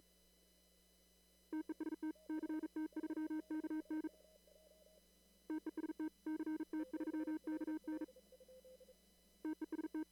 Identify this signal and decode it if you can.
Morse code, heard as a keyed tone, about 38 wpm, 310 Hz. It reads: TEST CT3KN TEST CT3KN TEST
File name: CW / kt5j wpx cw-op-k5pi